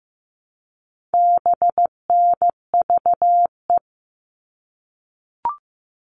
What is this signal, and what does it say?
Morse code, heard as a keyed tone, about 15 wpm, 700 Hz. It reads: BNVE
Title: All Cut Numbers - 4-Digit 15wpm